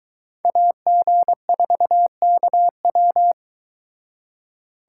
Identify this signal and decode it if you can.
Morse code, heard as a keyed tone, about 23 wpm, 700 Hz. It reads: AG4KW